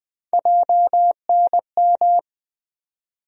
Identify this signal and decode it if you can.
Morse code, heard as a keyed tone, about 20 wpm, 700 Hz. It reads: JNM
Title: USA Call Sign Postfixes - 3 Letters 20wpm